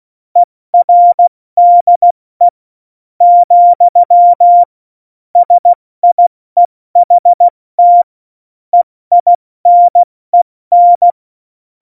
Morse code, heard as a keyed tone, about 16 wpm, 700 Hz. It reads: ERDE , SIEHT EINEN
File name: Morsekurs / Witze / Witz 14. August 2018, 70 F80